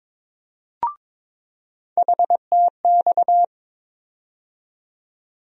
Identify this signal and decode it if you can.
Morse code, heard as a keyed tone, about 22 wpm, 700 Hz. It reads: HTX
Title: USA Call Sign Postfixes - 3 Letters 22wpm